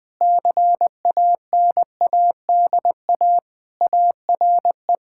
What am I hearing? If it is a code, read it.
Morse code, heard as a keyed tone, about 20 wpm, 700 Hz. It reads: CANADA ARE
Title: Sentences with 4-Word Phrases - 20wpm